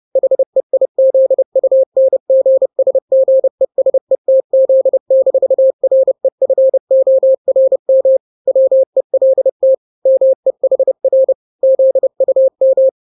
Morse code, heard as a keyed tone, about 30 wpm, 530 Hz. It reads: HEIZUNGSGESETZ-REFORM WELT MEHR ZUM
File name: daily / 29wpm